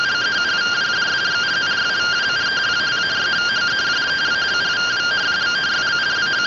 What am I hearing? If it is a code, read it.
Two-tone teleprinter signal, about 170 Hz shift.
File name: findings / 202 coronal iodine 3mm.gen-22 rtty rx